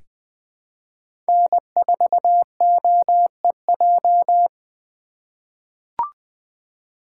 Morse code, heard as a keyed tone, about 20 wpm, 700 Hz. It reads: N4OEJ
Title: USA Call Signs - 5 Characters - 3x Character Spacing - 20wpm